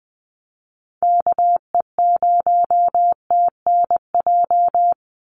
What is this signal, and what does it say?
Morse code, heard as a keyed tone, about 20 wpm, 700 Hz. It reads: KE0TNJ
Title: USA Call Signs 20wpm